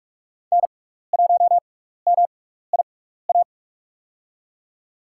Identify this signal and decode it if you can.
Morse code, heard as a keyed tone, about 42 wpm, spaced as for 15 wpm, 700 Hz. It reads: N1MIA
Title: USA Call Signs - 6x Character Spacing - 45wpm